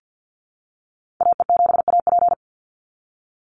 Morse code, heard as a keyed tone, about 50 wpm, 700 Hz. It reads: AE7AP
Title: NA SOTA Activator Callsigns 50wpm